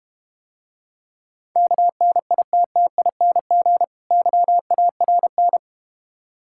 Morse code, heard as a keyed tone, about 32 wpm, 700 Hz. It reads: KNITTING YARN